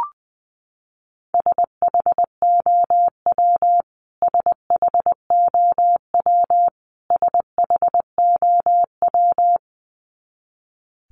Morse code, heard as a keyed tone, about 19 wpm, 700 Hz. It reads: SHOW SHOW SHOW
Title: Top 300 Words Repeated 20wpm